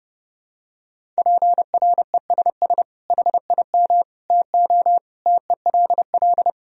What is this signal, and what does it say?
Morse code, heard as a keyed tone, about 30 wpm, 700 Hz. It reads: PRESS HIM TO TELL
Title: Sentences from Top 1000 Words - Review Part 1 - 30wpm